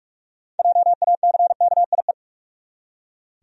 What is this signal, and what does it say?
Morse code, heard as a keyed tone, about 42 wpm, 700 Hz. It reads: JACKIE
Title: USA Common Names - 45wpm